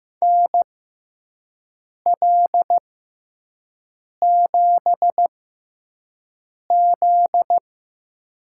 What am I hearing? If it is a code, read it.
Morse code, heard as a keyed tone, about 15 wpm, spaced as for 5 wpm, 700 Hz. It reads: NL7Z
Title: USA Call Signs - 4 Characters - 6x Character Spacing - 15wpm